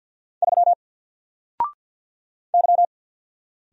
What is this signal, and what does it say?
Morse code, heard as a keyed tone, about 50 wpm, 700 Hz. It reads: 3 Y
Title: Single Letter-Number Alt 50wpm